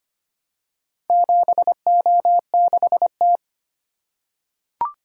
Morse code, heard as a keyed tone, about 25 wpm, 700 Hz. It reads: 7O6T
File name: practice / 25wpm / Four Letter-Number Combinations - 25wpm